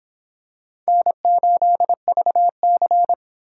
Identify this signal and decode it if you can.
Morse code, heard as a keyed tone, about 26 wpm, 700 Hz. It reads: N8VC